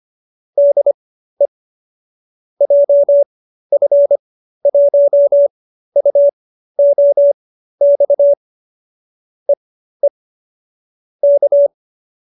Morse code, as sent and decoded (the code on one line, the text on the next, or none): -.. / .--- ..-. .---- ..- --- -..- / . / -.-
DE JF1UOX EE K